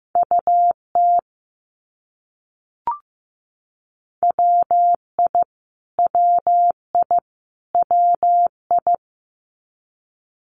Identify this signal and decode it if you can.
Morse code, heard as a keyed tone, about 15 wpm, 700 Hz. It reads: UT WI WI WI
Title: US States Repeated 3x 15wpm